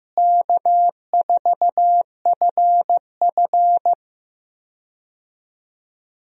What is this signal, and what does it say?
Morse code, heard as a keyed tone, about 15 wpm, 700 Hz. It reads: K4FF
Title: USA Call Signs - 4 Characters - Repeated 3x - 15wpm